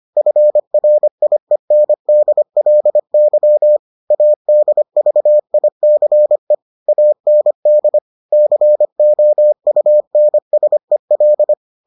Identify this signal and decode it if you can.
Morse code, heard as a keyed tone, about 25 wpm, 600 Hz. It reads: FRIENDLY ADVICE AND COUNSEL